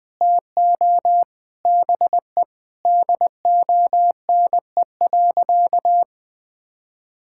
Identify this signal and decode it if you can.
Morse code, heard as a keyed tone, about 21 wpm, 700 Hz. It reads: TO BE DONE.
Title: Sentences with 7-Word Phrases - 20wpm